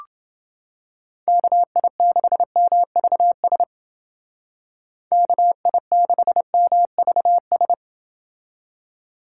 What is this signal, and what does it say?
Morse code, heard as a keyed tone, about 30 wpm, 700 Hz. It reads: KI6MVS KI6MVS